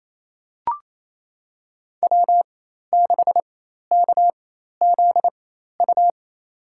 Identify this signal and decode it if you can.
Morse code, heard as a keyed tone, about 28 wpm, spaced as for 13 wpm, 700 Hz. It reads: W6KZU
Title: USA Call Signs - 4x Character Spacing - 28wpm